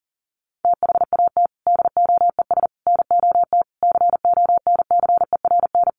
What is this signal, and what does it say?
Morse code, heard as a keyed tone, about 40 wpm, 700 Hz. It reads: THAT DOES NOT CONCERN
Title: Sentences from Top 1000 Words 40wpm